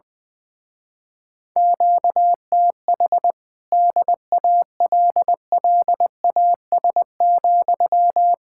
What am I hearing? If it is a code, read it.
Morse code, heard as a keyed tone, about 21 wpm, 700 Hz. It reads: QTH DALLAS,
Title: Speed-Racing Common QSO Phrases - 20wpm